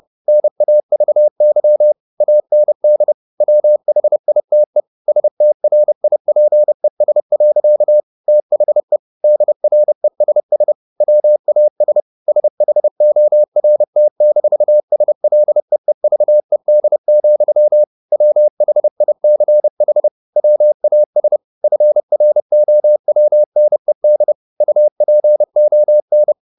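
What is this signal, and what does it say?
Morse code, heard as a keyed tone, about 30 wpm, 600 Hz. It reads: NAVY AND WHITE STRIPES. THE DRESS WAS SHORT-SLEEVED, WHICH WAS FROWNED UPON
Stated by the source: Morse code at 30 wpm, the novel Never by Ken Follett in English, from CW med Gnister.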